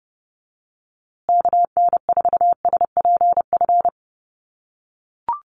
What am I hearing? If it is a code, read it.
Morse code, heard as a keyed tone, about 30 wpm, 700 Hz. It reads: KN4SPF
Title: USA Call Signs Repeated 3x 30wpm